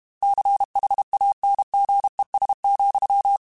Morse code, heard as a keyed tone, about 32 wpm, 800 Hz. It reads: CHANGES,
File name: Technical Subjects / Technical Subjects-mp3s / Reactance @30